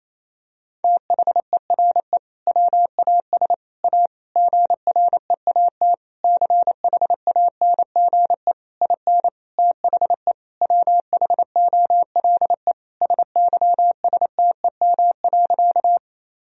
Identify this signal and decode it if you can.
Morse code, heard as a keyed tone, about 27 wpm, 700 Hz. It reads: THERE WAS A GREAT CHANGE IN THE WHOLE SYSTEM.
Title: Sentences from Top 1000 Words - Review Part 3 - 28wpm